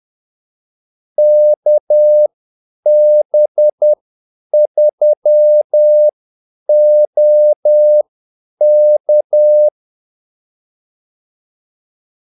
Morse code, heard as a keyed tone, about 10 wpm, spaced as for 8 wpm, 600 Hz. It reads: KB3OK
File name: Callsigns-10-8wpm